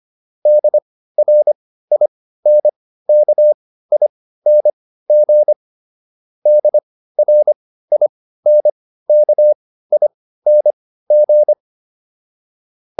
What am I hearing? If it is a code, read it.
Morse code, heard as a keyed tone, about 25 wpm, spaced as for 15 wpm, 600 Hz. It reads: DRINKING DRINKING